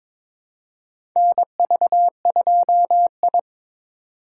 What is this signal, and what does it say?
Morse code, heard as a keyed tone, about 22 wpm, 700 Hz. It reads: NV2I
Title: USA Call Signs - 4 Characters - 3x Character Spacing - 22wpm